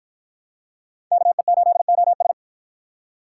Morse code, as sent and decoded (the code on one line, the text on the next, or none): -.- . ---.. --- ...
KE8OS